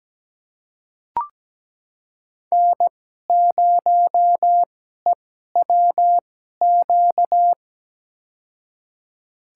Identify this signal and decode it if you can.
Morse code, heard as a keyed tone, about 17 wpm, spaced as for 12 wpm, 700 Hz. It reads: N0EWQ
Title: USA Call Signs - 2x Character Spacing - 17wpm